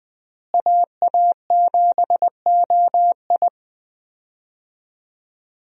Morse code, heard as a keyed tone, about 20 wpm, 700 Hz. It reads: AA7OI
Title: USA Call Signs - 5 Characters - Repeated 3x - 20wpm